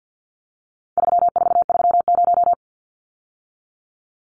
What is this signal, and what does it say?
Morse code, heard as a keyed tone, about 48 wpm, 700 Hz. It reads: 3430
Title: Numbers - 4-Digit 50wpm